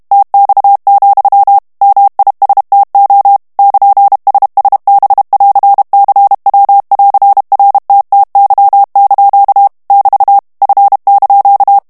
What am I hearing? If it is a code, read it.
Morse code, heard as a keyed tone, about 32 wpm, 800 Hz. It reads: TX, MISTO <KN>SSB<AR>CW<AR>RTTY) <BT> F)
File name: texte clair 2 vitesse 30 mots minute